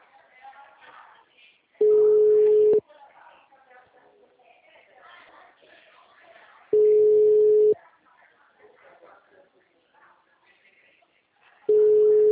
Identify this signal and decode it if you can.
Ringback tone (Europe, 425 Hz), with speech.